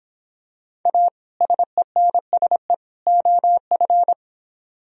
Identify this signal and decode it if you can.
Morse code, heard as a keyed tone, about 24 wpm, 700 Hz. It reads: A SENSE OF